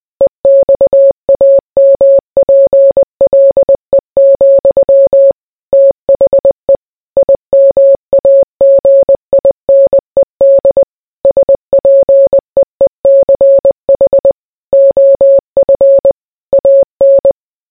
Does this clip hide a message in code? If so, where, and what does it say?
Yes: Morse code from 0.2 to 17.3 s, heard as a keyed tone, about 20 wpm, 550 Hz. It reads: EXAMPLE, THE IMAGINED SPEECH OF AN